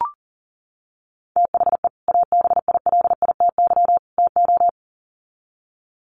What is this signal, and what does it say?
Morse code, heard as a keyed tone, about 40 wpm, 700 Hz. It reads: THE ABILITY TO
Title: Sets of 3 Words 40wpm